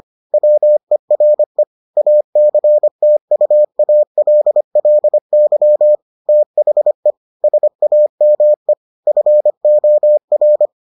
Morse code, heard as a keyed tone, about 25 wpm, 600 Hz. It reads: WERE ACTUALLY THE SAME FOR